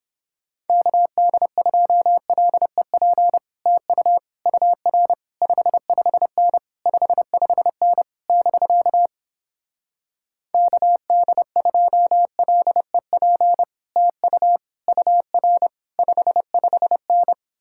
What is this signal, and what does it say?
Morse code, heard as a keyed tone, about 31 wpm, 700 Hz. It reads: KD2LEP TU UR 55N 55N <BK> KD2LEP TU UR 55N